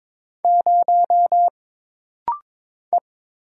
Morse code, heard as a keyed tone, about 22 wpm, 700 Hz. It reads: 0 E
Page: This rapid-fire timing is geared to push your ICR (Instant Character Recognition) skills.